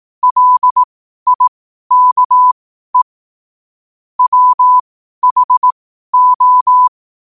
Morse code, heard as a keyed tone, about 18 wpm, spaced as for 12 wpm, 1000 Hz. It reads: LIKE WHO
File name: qotdfiles / 12wpm